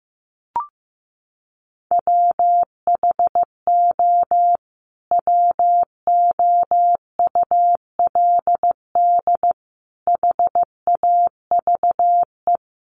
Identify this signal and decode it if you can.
Morse code, heard as a keyed tone, about 15 wpm, 700 Hz. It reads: WHO WOULD HAVE
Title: Sentences from Top 1200 Words 15wpm